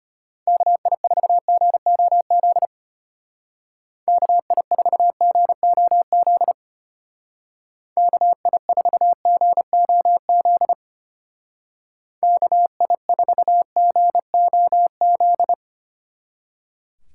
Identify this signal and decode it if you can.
Morse code, heard as a keyed tone, about 31 wpm, 700 Hz. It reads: KI4GOZ KI4GOZ KI4GOZ KI4GOZ
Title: Speed Racing - USA 2x3 Call Signs 25wpm